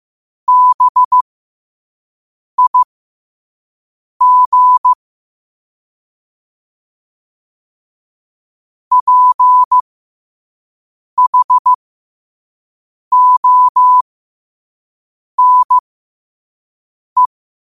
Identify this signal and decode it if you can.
Morse code, heard as a keyed tone, about 15 wpm, spaced as for 5 wpm, 1000 Hz. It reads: BIG PHONE
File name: qotdfiles / 5wpm